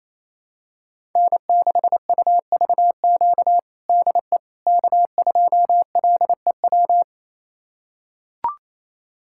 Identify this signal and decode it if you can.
Morse code, heard as a keyed tone, about 28 wpm, 700 Hz. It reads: N6UVQ DE K2LEW
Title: USA Callsign DE Callsign - 5 Characters - 28wpm